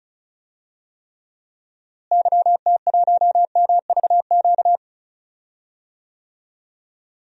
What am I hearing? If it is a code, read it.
Morse code, heard as a keyed tone, about 36 wpm, 700 Hz. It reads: YT1MVQ
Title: DXCC-100 Call Signs - 6 Characters - Repeated 3x - 35wpm